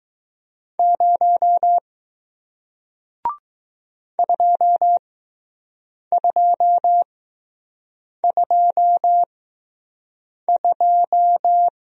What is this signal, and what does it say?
Morse code, heard as a keyed tone, about 20 wpm, spaced as for 6 wpm, 700 Hz. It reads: 0 2222